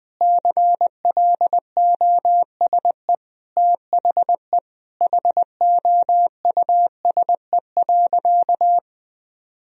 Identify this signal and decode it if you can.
Morse code, heard as a keyed tone, about 20 wpm, 700 Hz. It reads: CLOSE THE HOUSE.
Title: Sentences from Top 1000 Words - Review Part 1 - 20wpm